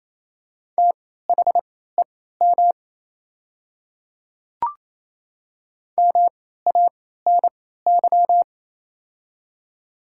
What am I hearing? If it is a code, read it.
Morse code, heard as a keyed tone, about 28 wpm, spaced as for 16 wpm, 690 Hz. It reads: THEM MANY
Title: 3x Character Spacing - Top 100 Words - 28wpm